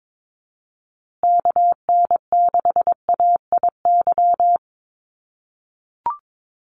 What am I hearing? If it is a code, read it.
Morse code, heard as a keyed tone, about 22 wpm, 700 Hz. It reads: KN6AIY
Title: USA Call Signs Repeated 2x 22wpm